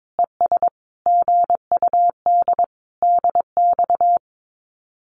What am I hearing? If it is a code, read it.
Morse code, heard as a keyed tone, about 22 wpm, 700 Hz. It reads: ES GUD DX
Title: Common QSO Phrases 22wpm